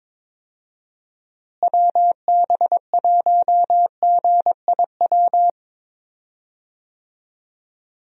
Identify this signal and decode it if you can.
Morse code, heard as a keyed tone, about 22 wpm, 700 Hz. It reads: WB1GIW
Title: USA Call Signs - 6 Characters - Repeated 3x - 22wpm